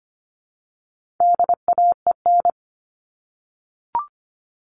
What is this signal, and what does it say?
Morse code, heard as a keyed tone, about 25 wpm, 700 Hz. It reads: DAEN
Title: All Cut Numbers - 4-Digit 25wpm